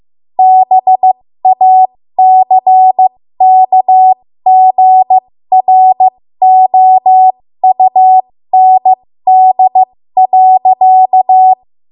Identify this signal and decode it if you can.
Morse code, heard as a keyed tone, about 15 wpm, spaced as for 13 wpm, 750 Hz. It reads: BACKGROUND.
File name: Morse / 13WPM